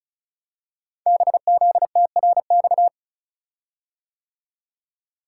Morse code, heard as a keyed tone, about 36 wpm, 700 Hz. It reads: BZTRX
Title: Code Groups - 5 Letter:Numbers - 35wpm